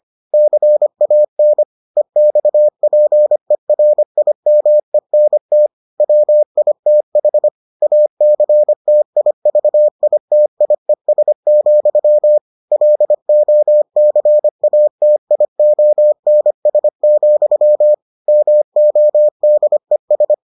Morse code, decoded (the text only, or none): CAN EXPERIMENT WITH ACTIVITIES, LOCATIONS, MODES